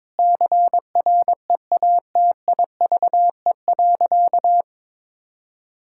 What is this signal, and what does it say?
Morse code, heard as a keyed tone, about 22 wpm, 700 Hz. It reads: CREATIVE.
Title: Sentences from Top 1000 Words - Review Part 5 - 22wpm